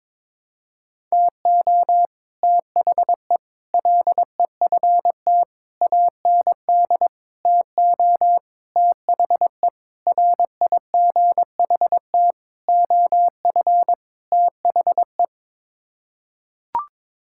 Morse code, heard as a keyed tone, about 22 wpm, 700 Hz. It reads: TO THE LEFT AND TO THE RIGHT OF THE
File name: Sets of 9 Words - 22wpm